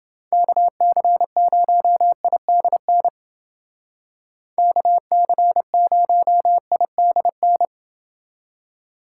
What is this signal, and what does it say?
Morse code, heard as a keyed tone, about 28 wpm, 700 Hz. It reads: KC0IDN KC0IDN